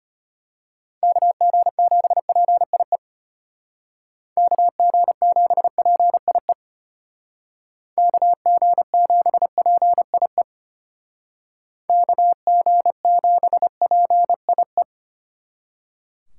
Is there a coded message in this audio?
Yes: Morse code from 1.0 to 14.8 s, heard as a keyed tone, about 31 wpm, 700 Hz. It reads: KG7PIE KG7PIE KG7PIE KG7PIE